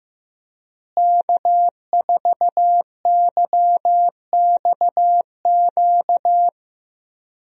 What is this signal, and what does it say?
Morse code, heard as a keyed tone, about 15 wpm, 700 Hz. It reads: K4YXQ
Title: USA Call Signs - 5 Characters - Repeated 3x - 15wpm